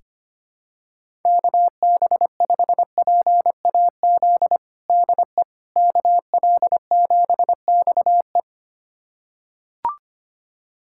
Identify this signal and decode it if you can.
Morse code, heard as a keyed tone, about 25 wpm, 700 Hz. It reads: KB5PAZ DE KL7XE
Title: USA Callsign DE Callsign 25wpm